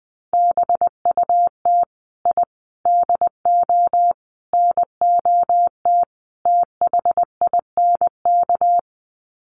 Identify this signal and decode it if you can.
Morse code, heard as a keyed tone, about 20 wpm, 700 Hz. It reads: BUT I DO NOT THINK
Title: Sets of 5 Words 20wpm